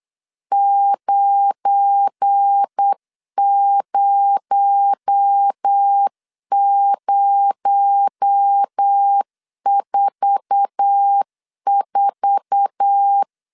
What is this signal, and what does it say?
Morse code, heard as a keyed tone, about 8 wpm, 800 Hz. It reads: 90044
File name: LEZIONI / 8 YJ45 / 8 YJ4 8wpm 3246